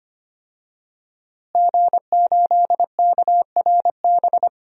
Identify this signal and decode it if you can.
Morse code, heard as a keyed tone, about 25 wpm, 700 Hz. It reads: G8KRB
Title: DXCC-100 Call Signs - 5 Characters - Repeated 2x - 25wpm